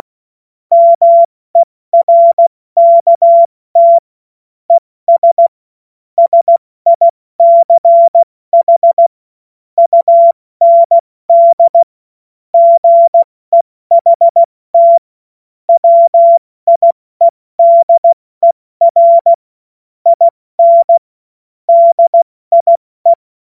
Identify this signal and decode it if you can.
Morse code, heard as a keyed tone, about 16 wpm, 680 Hz. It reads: MERKT ES SICH UND GEHT WIEDER IN DIE